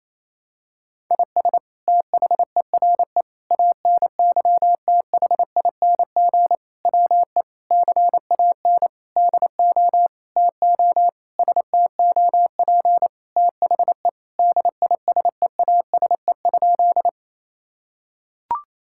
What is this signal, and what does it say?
Morse code, heard as a keyed tone, about 28 wpm, 700 Hz. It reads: IS THERE ANYTHING WE CAN DO TO STOP THE DISEASE?
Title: Sentences from Top 1000 Words - Review Part 6 - 28wpm